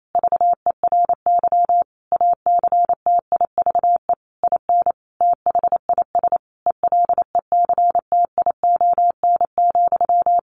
Morse code, heard as a keyed tone, about 28 wpm, 700 Hz. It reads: VERY ACTIVE IN THIS ELECTION,
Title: Sentences from Top 1500 Words 28wpm